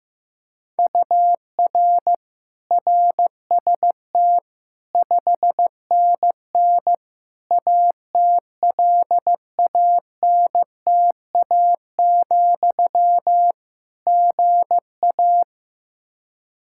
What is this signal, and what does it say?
Morse code, heard as a keyed tone, about 15 wpm, 700 Hz. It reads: UR RST 5NN ATLANTA, GA
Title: Practice QSO - Exchange 1 - Beginning - 15wpm